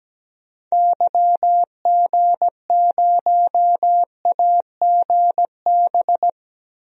Morse code, heard as a keyed tone, about 17 wpm, 700 Hz. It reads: YG0AGB
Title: DXCC-100 Call Signs - 6 Characters - 17wpm